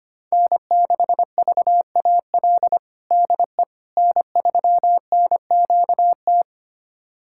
Morse code, heard as a keyed tone, about 25 wpm, 700 Hz. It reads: N6VAL DE N3NQT
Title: USA Callsign DE Callsign - 5 Characters - 25wpm